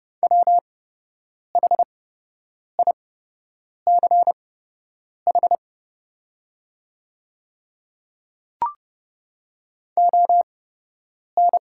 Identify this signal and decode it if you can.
Morse code, heard as a keyed tone, about 29 wpm, spaced as for 8 wpm, 700 Hz. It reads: WHICH ON
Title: 8x Character Spacing - Top 100 Words - 30wpm